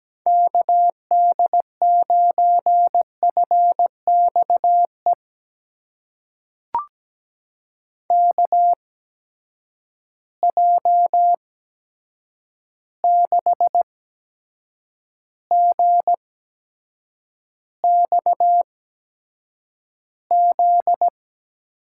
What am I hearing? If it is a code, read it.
Morse code, heard as a keyed tone, about 17 wpm, spaced as for 7 wpm, 700 Hz. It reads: KD9FXE KJ6GXZ
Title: USA Call Signs - 6 Characters - 8x Character Spacing - 17wpm